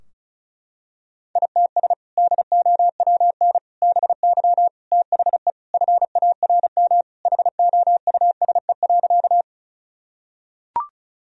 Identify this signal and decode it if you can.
Morse code, heard as a keyed tone, about 36 wpm, 700 Hz. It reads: ITS DOWN BY THE FARM HOUSE.
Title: Sentences from Top 1000 Words - Review Part 2 - 35wpm